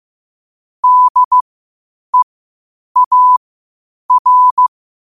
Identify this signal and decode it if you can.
Morse code, heard as a keyed tone, about 15 wpm, spaced as for 8 wpm, 1000 Hz. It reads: DEAR